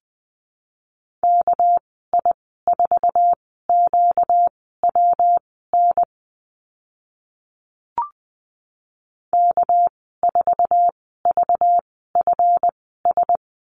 Morse code, heard as a keyed tone, about 19 wpm, spaced as for 14 wpm, 700 Hz. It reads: KI4QWN K4VFS
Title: 245 Lesson 39g - Letter z - Call Signs 20wpm